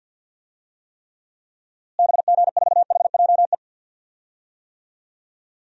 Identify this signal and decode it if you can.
Morse code, heard as a keyed tone, about 52 wpm, 700 Hz. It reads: BG4HJE
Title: DXCC-100 Call Signs - 6 Characters - Repeated 3x - 50wpm